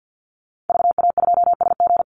Morse code, heard as a keyed tone, about 50 wpm, 700 Hz. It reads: VA2SG
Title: Call Signs Repeated 2x - 50wpm